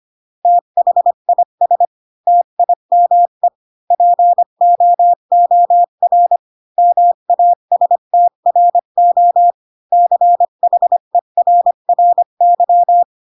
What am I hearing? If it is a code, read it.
Morse code, heard as a keyed tone, about 25 wpm, 700 Hz. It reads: THIS TIME POOR MASTRO CHERRY
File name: Pinocchio / Pinocchio ch1 20 25 ik1hge